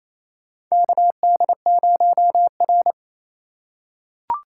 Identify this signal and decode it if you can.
Morse code, heard as a keyed tone, about 28 wpm, 700 Hz. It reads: KD0R
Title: USA Call Signs - 4 Characters - 2x Character Spacing - 28wpm